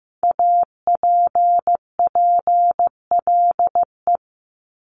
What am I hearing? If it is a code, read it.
Morse code, heard as a keyed tone, about 15 wpm, 700 Hz. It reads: APPLE
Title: Multi-letter Phonemes 15wpm